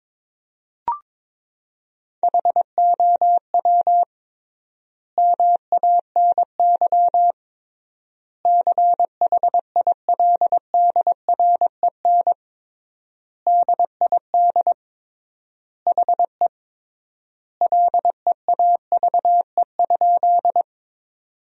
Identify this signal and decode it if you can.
Morse code, heard as a keyed tone, about 22 wpm, 700 Hz. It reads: HOW MANY CHILDREN DID HE LEAVE?
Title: 3x Word-Spacing - Sentences from Top 400 Words - 22wpm